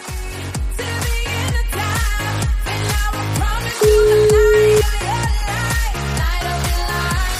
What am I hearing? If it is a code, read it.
Ringback tone (Europe, 425 Hz), with noise.